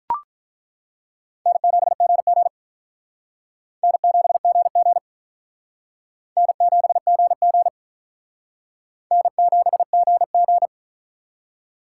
Morse code, heard as a keyed tone, about 48 wpm, 700 Hz. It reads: N7GG N7GG N7GG N7GG